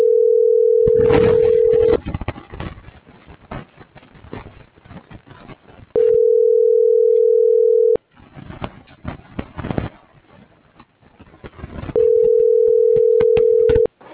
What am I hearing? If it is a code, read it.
Ringback tone (North America), 440 + 480 Hz.